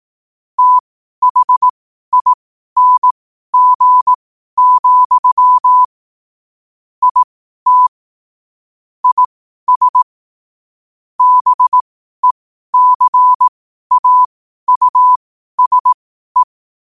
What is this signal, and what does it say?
Morse code, heard as a keyed tone, about 18 wpm, spaced as for 12 wpm, 1000 Hz. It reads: THING, IT IS BECAUSE
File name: qotdfiles / 12wpm